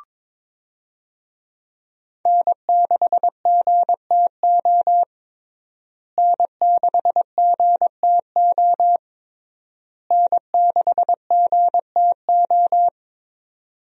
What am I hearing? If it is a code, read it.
Morse code, heard as a keyed tone, about 22 wpm, 700 Hz. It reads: N6GTO N6GTO N6GTO
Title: USA Call Signs - 5 Characters - Repeated 3x - 22wpm